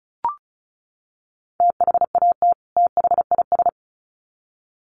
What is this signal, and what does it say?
Morse code, heard as a keyed tone, about 34 wpm, 700 Hz. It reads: THAT THIS
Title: Sets of 2 Words 35wpm